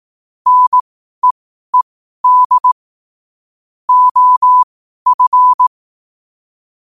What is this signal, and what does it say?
Morse code, heard as a keyed tone, about 18 wpm, spaced as for 12 wpm, 1000 Hz. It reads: NEED OF
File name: 12wpm